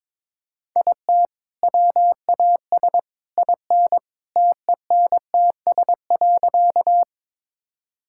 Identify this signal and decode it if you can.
Morse code, heard as a keyed tone, about 22 wpm, 700 Hz. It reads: IT WAS IN TENTS.
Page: morse-code-ninja-practice-22wpm